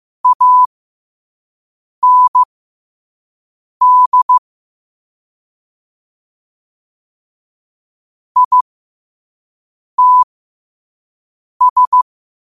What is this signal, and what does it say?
Morse code, heard as a keyed tone, about 15 wpm, spaced as for 5 wpm, 1000 Hz. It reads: AND ITS
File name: qotdfiles / 5wpm